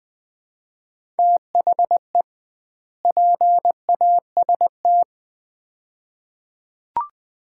Morse code, heard as a keyed tone, about 20 wpm, 700 Hz. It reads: THE PAST
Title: 2x Word-Spacing - Sets of 2 Words - 20wpm